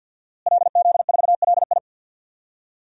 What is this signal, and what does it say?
Morse code, heard as a keyed tone, about 48 wpm, 700 Hz. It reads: LZ4LI
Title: DXCC-100 Call Signs - 5 Characters - Repeated 3x - 50wpm